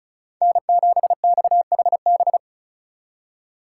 Morse code, heard as a keyed tone, about 43 wpm, 700 Hz. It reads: N7XHB
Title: Speed Racing - USA Call Signs 35wpm